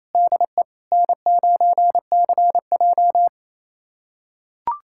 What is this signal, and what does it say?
Morse code, heard as a keyed tone, about 28 wpm, 700 Hz. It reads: DE N9CJ
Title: USA Callsign DE Callsign - 4 Characters - 28wpm